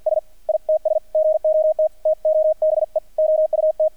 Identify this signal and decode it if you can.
Morse code, heard as a keyed tone, about 48 wpm, 630 Hz, with radio noise. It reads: S ITS GOT TOBE OUT
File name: archived / CRSnet